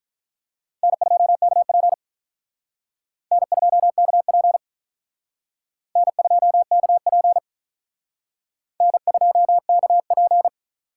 Morse code, heard as a keyed tone, about 44 wpm, 700 Hz. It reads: N2KP N2KP N2KP N2KP